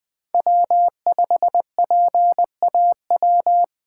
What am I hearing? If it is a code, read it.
Morse code, heard as a keyed tone, about 20 wpm, 700 Hz. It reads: W5PAW
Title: USA Call Signs - 5 Characters - 4x Character Spacing - 20wpm